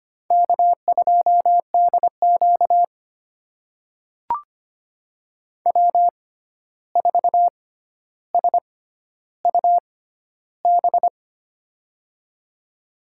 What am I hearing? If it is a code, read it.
Morse code, heard as a keyed tone, about 25 wpm, spaced as for 12 wpm, 700 Hz. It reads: K2DQ W4SUB
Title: USA Call Signs - 6x Character Spacing - 25wpm